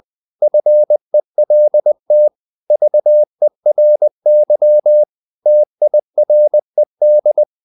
Morse code, heard as a keyed tone, about 20 wpm, 600 Hz. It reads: FELT VERY TIRED
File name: the old man and the sea / the old man and the sea part 1 20wpm